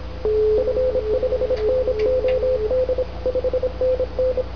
Morse code, heard as a keyed tone, about 26 wpm, 510 Hz, with radio noise. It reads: F5NOD 5NN